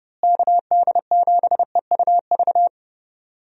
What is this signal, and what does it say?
Morse code, heard as a keyed tone, about 32 wpm, 700 Hz. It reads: KD7EUV
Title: USA Call Signs - 6 Characters - Repeated 3x - 30wpm